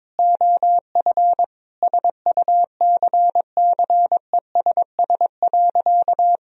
Morse code, heard as a keyed tone, about 21 wpm, 700 Hz. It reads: OF SUCCESS.
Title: Sentences from Top 1000 Words - Review Part 2 - 22wpm